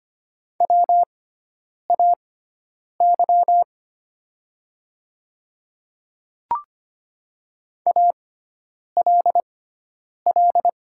Morse code, heard as a keyed tone, about 25 wpm, spaced as for 9 wpm, 700 Hz. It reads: WAY ALL
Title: 6x Character Spacing - Top 100 Words - 25wpm